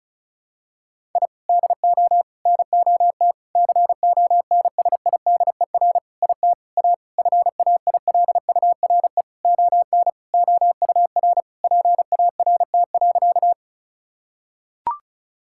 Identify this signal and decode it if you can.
Morse code, heard as a keyed tone, about 34 wpm, 700 Hz. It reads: I DO NOT CONSIDER IT A FAILURE ON OUR PART.
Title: Sentences from Top 1000 Words - Review Part 6 - 35wpm